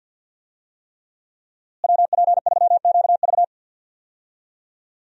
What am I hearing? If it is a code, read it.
Morse code, heard as a keyed tone, about 51 wpm, 700 Hz. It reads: WP3QV
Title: USA Call Signs - 5 Characters - Repeated 3x - 50wpm